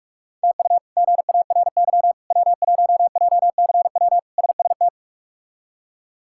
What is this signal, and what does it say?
Morse code, heard as a keyed tone, about 46 wpm, 700 Hz. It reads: TU GARY W1JCW SST
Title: K1USN SST QSO Practice - 45wpm